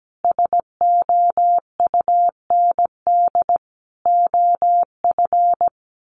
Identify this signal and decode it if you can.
Morse code, heard as a keyed tone, about 17 wpm, 700 Hz. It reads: SOUND OF
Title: Sentences from Top 1100 Words 17wpm